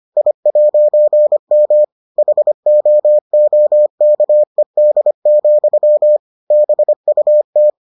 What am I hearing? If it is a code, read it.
Morse code, heard as a keyed tone, about 25 wpm, 600 Hz. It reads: I'M HOOKED, BUT